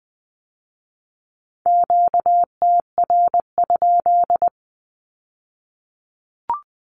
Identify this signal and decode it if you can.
Morse code, heard as a keyed tone, about 20 wpm, 700 Hz. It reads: QTR?
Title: Q-Codes with Spoken Meaning - 20wpm